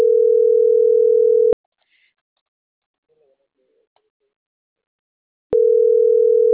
Ringback tone (North America), 440 + 480 Hz.